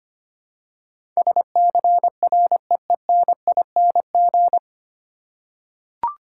Morse code, heard as a keyed tone, about 25 wpm, 700 Hz. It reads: SCREENING
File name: Nine Letter Words - 25wpm